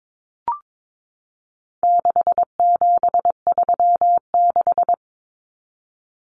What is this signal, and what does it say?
Morse code, heard as a keyed tone, about 22 wpm, 700 Hz. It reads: 6736
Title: Common Cut Numbers - 4-Digit 22wpm